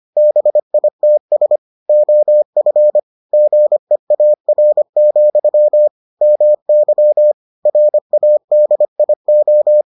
Morse code, heard as a keyed tone, about 25 wpm, 600 Hz. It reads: BITS OF GEAR, MY RADIO